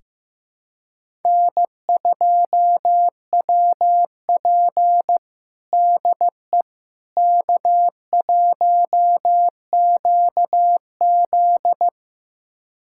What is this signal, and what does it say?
Morse code, heard as a keyed tone, about 15 wpm, 700 Hz. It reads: N2WP DE K1QZ
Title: USA Callsign DE Callsign - 4 Characters - 15wpm